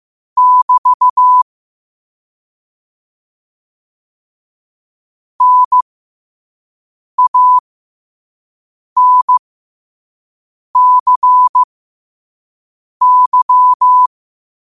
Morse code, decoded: <BT> NANCY

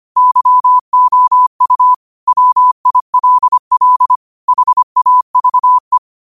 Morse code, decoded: YOU WILL HAVE

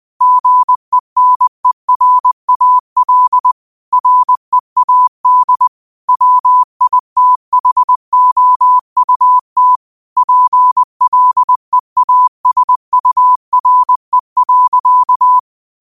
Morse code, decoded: GENERAL READ WITHOUT PLEASURE.

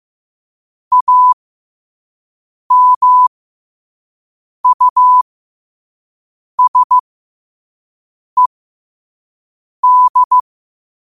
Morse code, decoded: AMUSED

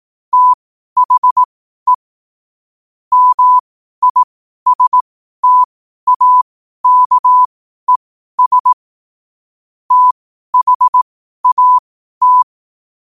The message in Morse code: THE MISTAKES THAT

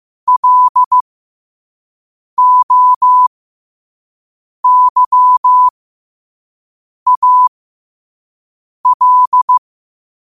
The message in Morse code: LOYAL

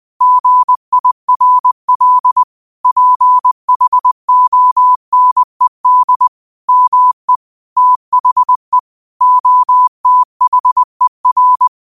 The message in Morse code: GIRL PHONED ME THE OTHER

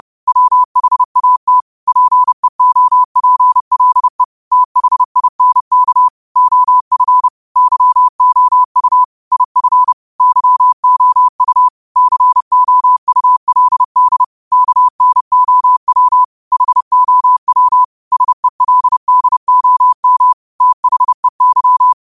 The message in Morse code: WHAT PEOPLE THINK OF YOU IF YOU COULD KNOW HOW SELDOM THEY

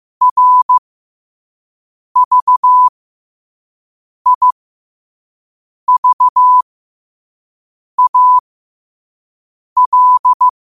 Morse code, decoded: RVIVAL

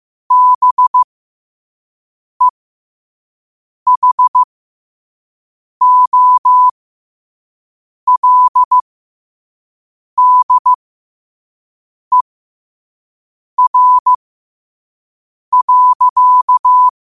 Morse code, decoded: BEHOLDER.

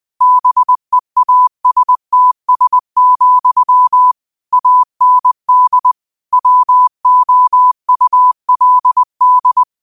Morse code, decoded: BEASTS, AND WOULD

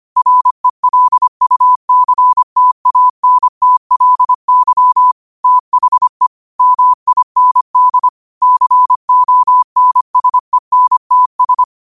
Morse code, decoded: RELUCTANTLY THE MIND CONSENTS

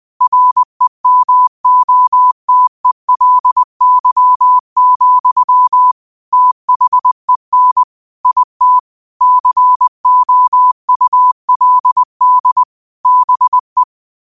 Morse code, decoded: REMOTELY, THEN IT COULD BE

